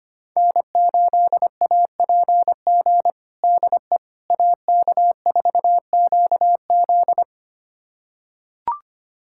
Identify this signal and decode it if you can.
Morse code, heard as a keyed tone, about 25 wpm, 700 Hz. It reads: N8APG DE AK4QZ
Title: USA Callsign DE Callsign - 5 Characters - 25wpm